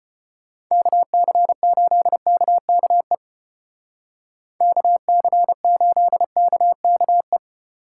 Morse code, heard as a keyed tone, about 30 wpm, 700 Hz. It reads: KC8KKE KC8KKE
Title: Speed Racing - USA 2x3 Call Signs 25wpm